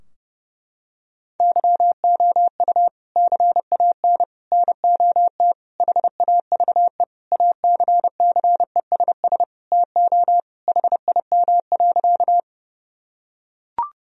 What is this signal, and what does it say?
Morse code, heard as a keyed tone, about 30 wpm, 700 Hz. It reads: YOU CAN NOT HAVE ACCESS TO HIM.
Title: Sentences from Top 1000 Words - Review Part 4 - 30wpm